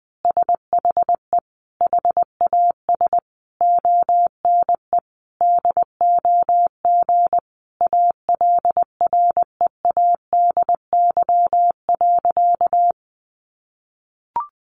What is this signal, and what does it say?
Morse code, heard as a keyed tone, about 20 wpm, 700 Hz. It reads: SHE HAS ONE DOG ALREADY.
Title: Easy Phrases 20wpm